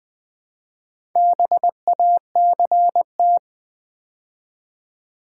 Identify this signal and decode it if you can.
Morse code, heard as a keyed tone, about 20 wpm, 700 Hz. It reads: BACT
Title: Build A Word - 2 Letters At A Time - 8-Letter Words - 20wpm